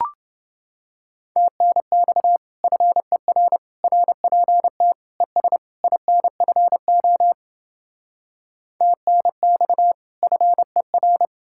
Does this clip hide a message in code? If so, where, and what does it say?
Yes: Morse code from 1.4 to 11.3 s, heard as a keyed tone, about 29 wpm, 700 Hz. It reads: TNX FER RPT ES INFO TNX FER